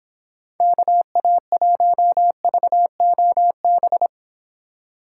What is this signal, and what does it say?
Morse code, heard as a keyed tone, about 26 wpm, 700 Hz. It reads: KA1VOB